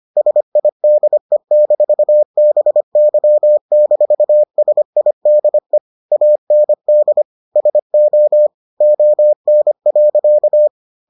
Morse code, heard as a keyed tone, about 25 wpm, 600 Hz. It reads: SIDE-BY-SIDE AND SO ON.